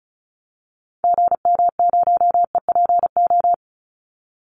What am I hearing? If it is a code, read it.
Morse code, heard as a keyed tone, about 34 wpm, 700 Hz. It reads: GM0EPO
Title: Call Signs - 35wpm